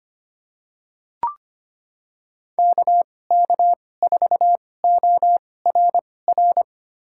Morse code, heard as a keyed tone, about 25 wpm, spaced as for 18 wpm, 700 Hz. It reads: KK4ORR
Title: USA Call Signs - 2x Character Spacing - 25wpm